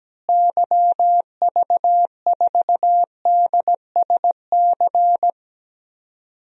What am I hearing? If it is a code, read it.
Morse code, heard as a keyed tone, about 17 wpm, 700 Hz. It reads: YV4DSC
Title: DXCC-100 Call Signs - 6 Characters - Repeated 2x - 17wpm